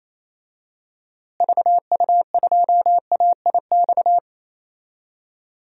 Morse code, heard as a keyed tone, about 28 wpm, 700 Hz. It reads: VU2AIX
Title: DXCC-100 Call Signs - 6 Characters - Repeated 2x - 28wpm